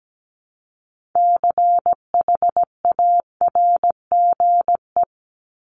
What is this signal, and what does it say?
Morse code, heard as a keyed tone, about 17 wpm, 700 Hz. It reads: CHARGE
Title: Top 751-1000 Words 17wpm